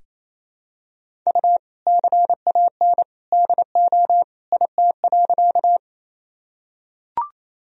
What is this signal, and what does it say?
Morse code, heard as a keyed tone, about 27 wpm, 700 Hz. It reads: U CAN DO IT.
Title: Sentences from Top 100 Words and Abbreviations 28wpm